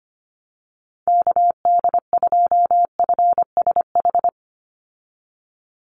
Morse code, heard as a keyed tone, about 25 wpm, 700 Hz. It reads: KD2FSH
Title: Call Signs Repeated 2x - 25wpm